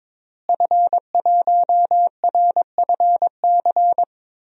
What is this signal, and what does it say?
Morse code, heard as a keyed tone, about 22 wpm, 700 Hz. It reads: F1RFC
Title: DXCC-100 Call Signs - 5 Characters - Repeated 3x - 22wpm